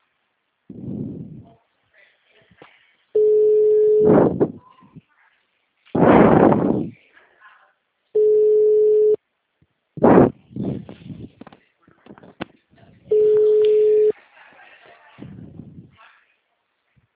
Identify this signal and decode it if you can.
Ringback tone (Europe, 425 Hz), with speech.
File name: out- -2011- - - .82078